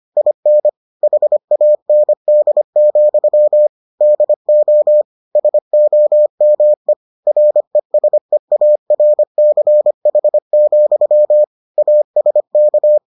Morse code, heard as a keyed tone, about 25 wpm, 600 Hz. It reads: IN HAND, DO SOME RESEARCH, ASK